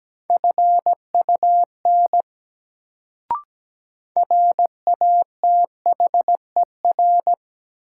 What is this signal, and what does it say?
Morse code, heard as a keyed tone, about 17 wpm, 700 Hz. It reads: FUN RATHER
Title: Sentences from Top 700 Words - New Word Drill - 17wpm